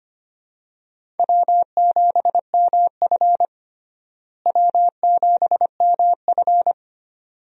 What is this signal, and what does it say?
Morse code, heard as a keyed tone, about 25 wpm, 700 Hz. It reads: W7MF W7MF
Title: USA Call Signs - 4 Characters - Repeated 2x - 25wpm